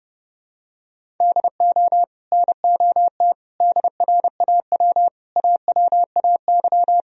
Morse code, heard as a keyed tone, about 30 wpm, 700 Hz. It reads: DO NOT DRAW AWAY
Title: Sentences from Top 1000 Words - Review Part 1 - 30wpm